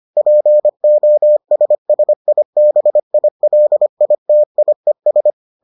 Morse code, heard as a keyed tone, about 25 wpm, 600 Hz. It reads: POSSIBILITIES